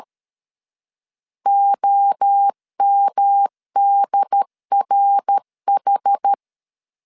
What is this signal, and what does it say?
Morse code, heard as a keyed tone, about 13 wpm, 800 Hz. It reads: OMDRH
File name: CW / DET / 1 3 1500C S B 12wpm 2828